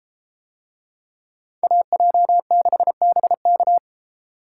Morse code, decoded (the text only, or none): AJ6BK